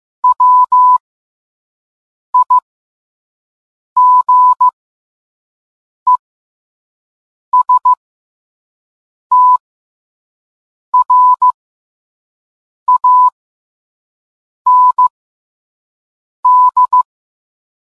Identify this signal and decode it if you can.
Morse code, heard as a keyed tone, about 15 wpm, spaced as for 5 wpm, 1000 Hz. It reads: WIGESTRAND